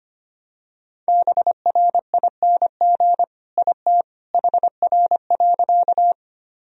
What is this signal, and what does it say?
Morse code, heard as a keyed tone, about 25 wpm, 700 Hz. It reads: BRING IT HR.
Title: Sentences from Top 500 Words and Abbreviations 25wpm